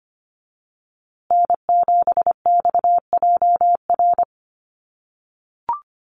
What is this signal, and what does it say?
Morse code, heard as a keyed tone, about 25 wpm, 700 Hz. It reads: N7XJR
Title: USA Call Signs Repeated 3x 25wpm